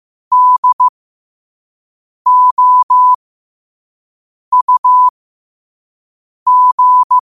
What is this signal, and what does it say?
Morse code, heard as a keyed tone, about 15 wpm, spaced as for 5 wpm, 1000 Hz. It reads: DOUG
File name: qotdfiles / 5wpm